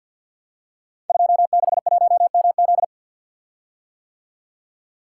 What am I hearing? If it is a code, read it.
Morse code, heard as a keyed tone, about 52 wpm, 700 Hz. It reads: 261MZ
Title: Code Groups - 5 Letter:Numbers - 50wpm